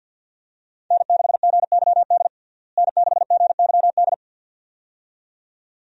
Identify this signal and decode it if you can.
Morse code, heard as a keyed tone, about 53 wpm, 700 Hz. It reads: N6GYD N6GYD
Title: USA Call Signs - 5 Characters - Repeated 2x - 50wpm